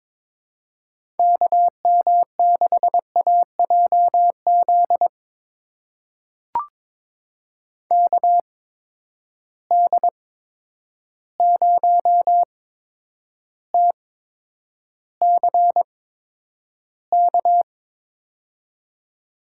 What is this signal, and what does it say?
Morse code, heard as a keyed tone, about 22 wpm, spaced as for 9 wpm, 700 Hz. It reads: KM6AJZ KD0TCK
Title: USA Call Signs - 6 Characters - 8x Character Spacing - 22wpm